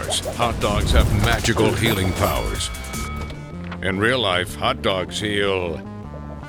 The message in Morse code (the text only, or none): none